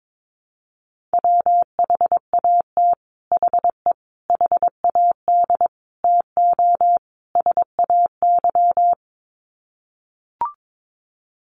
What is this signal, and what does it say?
Morse code, heard as a keyed tone, about 22 wpm, 700 Hz. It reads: WHAT HE HAD TO SAY
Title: Sets of 5 Words Encore2 22wpm